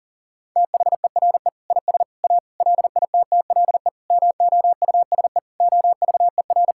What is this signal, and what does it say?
Morse code, heard as a keyed tone, about 40 wpm, 700 Hz. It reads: THERE IS A LITTLE MOUSE OVER
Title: Sentences from Top 1000 Words - Review Part 5 - 40wpm